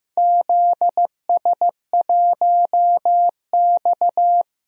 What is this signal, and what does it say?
Morse code, heard as a keyed tone, about 15 wpm, 700 Hz. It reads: ZS1X
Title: DXCC-100 Call Signs - 4 Characters - Repeated 2x - 15wpm